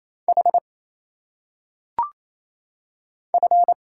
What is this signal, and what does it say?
Morse code, heard as a keyed tone, about 27 wpm, 700 Hz. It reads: H F
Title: Single Letters - Phonetic Pronunciation 28wpm